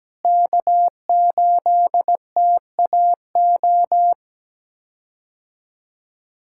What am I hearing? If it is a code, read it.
Morse code, heard as a keyed tone, about 17 wpm, 700 Hz. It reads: K8TAO
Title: K1USN SST QSO Practice - 17wpm